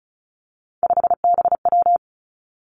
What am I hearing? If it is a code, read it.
Morse code, heard as a keyed tone, about 36 wpm, 700 Hz. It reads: 5BW
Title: Three Challenging Letter-Number Combinations 35wpm